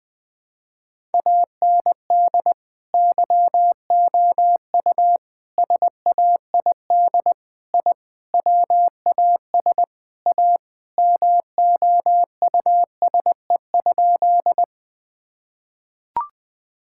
Morse code, heard as a keyed tone, about 20 wpm, 700 Hz. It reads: AND YOU SAID I WAS A MOUSE?
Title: Sentences from Top 1000 Words - Review Part 5 - 20wpm